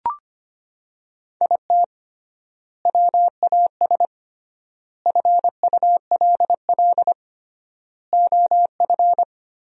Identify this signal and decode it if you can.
Morse code, heard as a keyed tone, about 25 wpm, 700 Hz. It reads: IT WAS FULL OF
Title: 3x Word-Spacing - Sentences from Top 500 Words - 25wpm